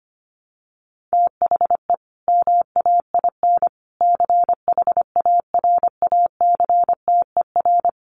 Morse code, heard as a keyed tone, about 25 wpm, 700 Hz. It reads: THE MAIN CHARACTER
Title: Additional Easy Phrases 25wpm